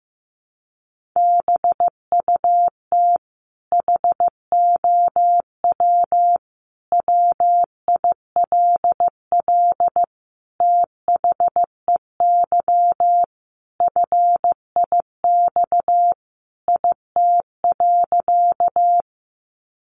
Morse code, heard as a keyed tone, about 15 wpm, 700 Hz. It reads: BUT HOW WILL THEY FIX IT.
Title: Sentences from Top 1400 Words 15wpm